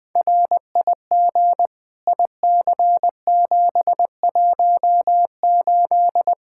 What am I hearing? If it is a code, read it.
Morse code, heard as a keyed tone, about 20 wpm, 700 Hz. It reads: RIG IC718
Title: Practice QSO - Exchange 2 - Info - 20wpm